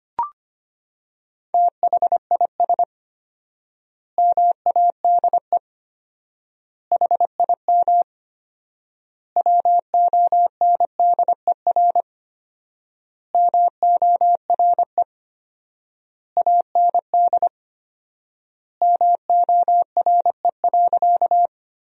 Morse code, heard as a keyed tone, about 25 wpm, 700 Hz. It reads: THIS MADE HIM WONDER MORE AND MORE.
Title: 4x Word-Spacing - Sentences from Top 500 Words - 25wpm